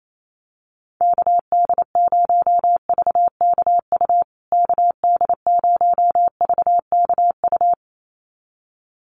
Morse code, heard as a keyed tone, about 28 wpm, 700 Hz. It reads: KD0VKU KD0VKU
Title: USA Call Signs Repeated 2x 28wpm